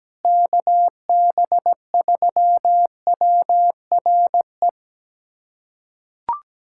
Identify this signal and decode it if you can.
Morse code, heard as a keyed tone, about 17 wpm, 700 Hz. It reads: KB3WRE
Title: USA Callsign DE Callsign 17wpm